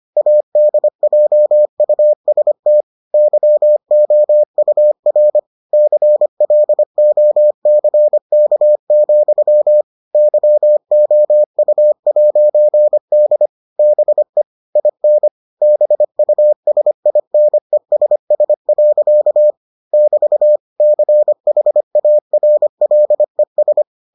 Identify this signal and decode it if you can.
Morse code, heard as a keyed tone, about 25 wpm, 600 Hz. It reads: ADJUST YOUR CLOCK, YOU'D BE IN BUSINESS. <BT> CHARLES